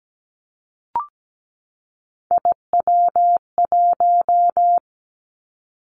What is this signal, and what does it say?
Morse code, heard as a keyed tone, about 17 wpm, 700 Hz. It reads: IW1
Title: Callsign Prefixes for Top 50 DXCC - 17wpm